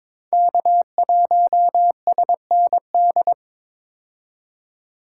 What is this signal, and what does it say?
Morse code, heard as a keyed tone, about 22 wpm, 700 Hz. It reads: K1SND
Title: K1USN SST QSO Practice - 22wpm